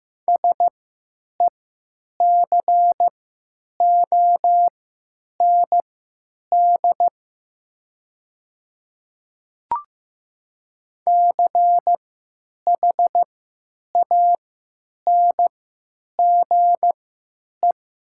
Morse code, heard as a keyed tone, about 15 wpm, spaced as for 9 wpm, 700 Hz. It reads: SECOND CHANGE
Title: 3x Character Spacing - Top 500 Words - 15wpm